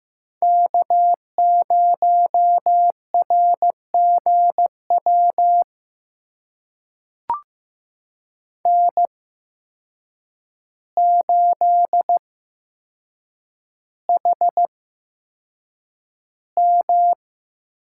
Morse code, heard as a keyed tone, about 15 wpm, spaced as for 7 wpm, 700 Hz. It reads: K0RGW N8HM
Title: USA Call Signs - 5 Characters - 8x Character Spacing - 15wpm